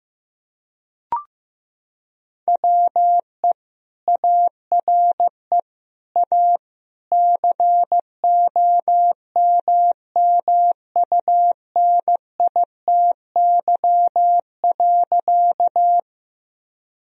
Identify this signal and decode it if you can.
Morse code, heard as a keyed tone, about 16 wpm, 700 Hz. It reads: WE ARE A COMMUNITY.
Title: Sentences from Top 1000 Words - Review Part 4 - 15wpm